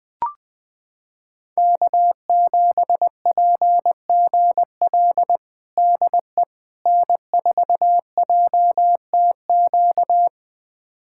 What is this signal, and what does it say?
Morse code, heard as a keyed tone, about 20 wpm, 700 Hz. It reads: K7PGL DE N4JTQ
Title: USA Callsign DE Callsign - 5 Characters - 20wpm